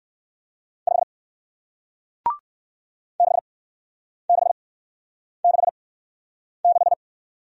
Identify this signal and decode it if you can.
Morse code, heard as a keyed tone, about 57 wpm, spaced as for 9 wpm, 700 Hz. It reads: 5 6666